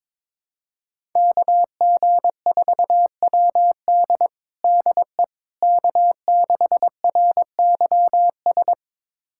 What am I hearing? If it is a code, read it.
Morse code, heard as a keyed tone, about 22 wpm, 700 Hz. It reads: KG4WD DE K6RYS
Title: USA Callsign DE Callsign - 5 Characters - 22wpm